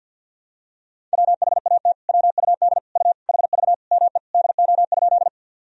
Morse code, heard as a keyed tone, about 48 wpm, 700 Hz. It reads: WHAT WUD U HV ME DO?
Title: Sentences from Top 200 Words and Abbreviations 50wpm